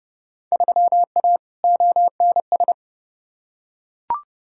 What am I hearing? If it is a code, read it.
Morse code, heard as a keyed tone, about 30 wpm, 700 Hz. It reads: 3A ONS
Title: Field Day Exchanges - 30wpm